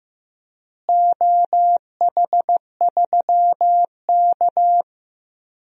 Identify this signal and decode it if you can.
Morse code, heard as a keyed tone, about 15 wpm, 700 Hz. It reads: OH3K
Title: DXCC-100 Call Signs - 4 Characters - Repeated 3x - 15wpm